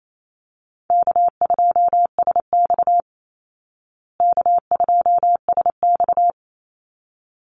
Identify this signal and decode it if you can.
Morse code, heard as a keyed tone, about 28 wpm, 690 Hz. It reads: K2SX K2SX
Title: Call Signs Repeated 3x - 28wpm